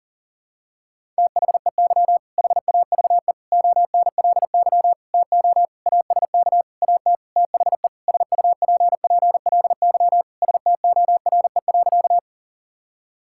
Morse code, heard as a keyed tone, about 40 wpm, 700 Hz. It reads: THEY HAVE ONLY TO ASK AT THE SUPPLY STORE.
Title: Sentences from Top 1000 Words - Review Part 5 - 40wpm